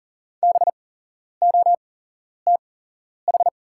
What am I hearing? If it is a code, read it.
Morse code, heard as a keyed tone, about 40 wpm, spaced as for 11 wpm, 690 Hz. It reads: BOTH
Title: 8x Character Spacing - Top 500 Words - 40wpm